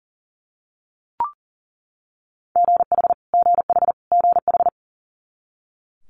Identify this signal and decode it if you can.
Morse code, heard as a keyed tone, about 38 wpm, 700 Hz. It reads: GH GH GH
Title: Multi-letter Phonemes Repeated 3x 40wpm